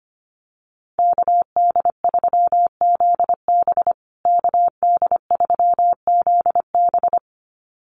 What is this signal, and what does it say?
Morse code, heard as a keyed tone, about 25 wpm, 700 Hz. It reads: KD3ZB KD3ZB
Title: USA Call Signs Repeated 2x 25wpm